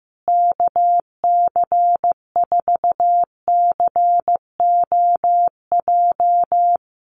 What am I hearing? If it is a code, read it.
Morse code, heard as a keyed tone, about 15 wpm, 700 Hz. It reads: KC4COJ
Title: USA Call Signs Repeated 2x 15wpm